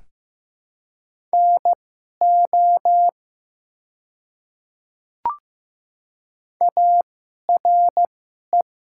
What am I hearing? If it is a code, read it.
Morse code, heard as a keyed tone, about 15 wpm, spaced as for 11 wpm, 700 Hz. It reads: NO ARE
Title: 2x Character Spacing - Top 300 Words - 15wpm